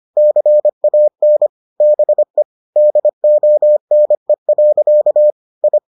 Morse code, heard as a keyed tone, about 25 wpm, 600 Hz. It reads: CAN BE DONE. I